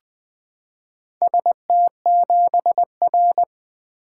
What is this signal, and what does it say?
Morse code, heard as a keyed tone, about 20 wpm, 700 Hz. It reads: ST7R
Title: Four Letter-Number Combinations - 20wpm